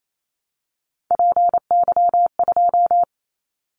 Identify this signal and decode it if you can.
Morse code, heard as a keyed tone, about 28 wpm, 700 Hz. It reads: PY2
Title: Callsign Prefixes for Top 50 DXCC - 28wpm